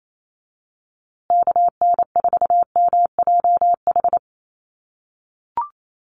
Morse code, heard as a keyed tone, about 28 wpm, 700 Hz. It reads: KN4MJH
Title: USA Call Signs Repeated 2x 28wpm